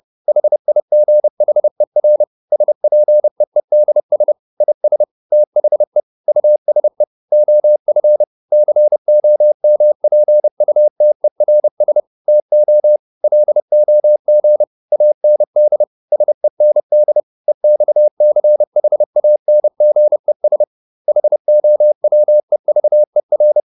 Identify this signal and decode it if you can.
Morse code, heard as a keyed tone, about 30 wpm, 600 Hz. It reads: HIGHER SPEEDS IS THE USE OF COMPUTERS TO LOG AND SEND EXCHANGES HOWEVER